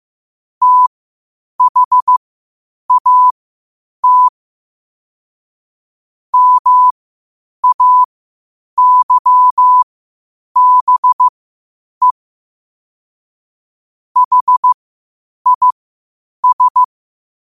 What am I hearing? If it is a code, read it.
Morse code, heard as a keyed tone, about 15 wpm, spaced as for 8 wpm, 1000 Hz. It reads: THAT MAYBE HIS
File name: qotdfiles / 8wpm